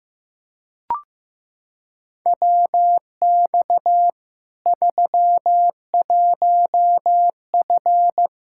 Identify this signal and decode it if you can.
Morse code, heard as a keyed tone, about 15 wpm, 700 Hz. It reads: WX 31F
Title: Practice QSO - Exchange 2 - Info - 15wpm